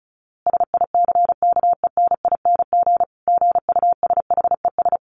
Morse code, heard as a keyed tone, about 34 wpm, 700 Hz. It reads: SICKENING GUSHES
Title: A Sound Of Thunder 35wpm